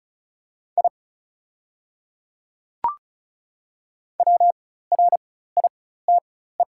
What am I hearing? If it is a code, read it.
Morse code, heard as a keyed tone, about 34 wpm, spaced as for 16 wpm, 700 Hz. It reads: I WRITE
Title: 4x Character Spacing - Top 100 Words - 35wpm